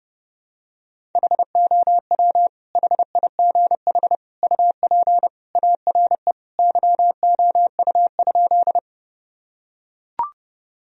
Morse code, heard as a keyed tone, about 32 wpm, 700 Hz. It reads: HOW HIGH UP ARE YOU?
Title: Sentences from Top 1000 Words - Review Part 1 - 30wpm